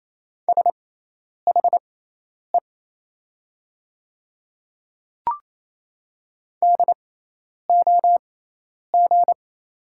Morse code, heard as a keyed tone, about 28 wpm, spaced as for 10 wpm, 700 Hz. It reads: SHE DOG
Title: 6x Character Spacing - Top 500 Words - 28wpm